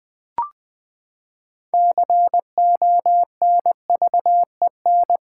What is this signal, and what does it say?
Morse code, heard as a keyed tone, about 20 wpm, 700 Hz. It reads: CONVEN
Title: Build A Word - 6 Letters At A Time - 12-Letter Words - 20wpm